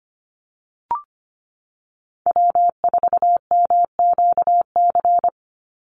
Morse code, heard as a keyed tone, about 25 wpm, 700 Hz. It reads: W4MQC
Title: Call Signs - 25wpm